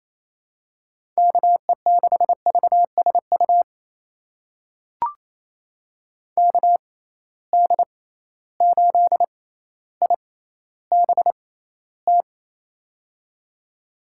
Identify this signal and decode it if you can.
Morse code, heard as a keyed tone, about 28 wpm, spaced as for 14 wpm, 700 Hz. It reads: KE6VSU KD8IBT
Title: USA Call Signs - 6 Characters - 6x Character Spacing - 28wpm